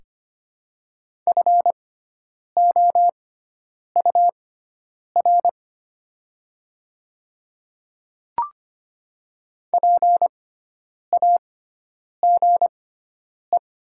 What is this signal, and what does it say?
Morse code, heard as a keyed tone, about 25 wpm, spaced as for 9 wpm, 700 Hz. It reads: FOUR PAGE
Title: 6x Character Spacing - Top 300 Words - 25wpm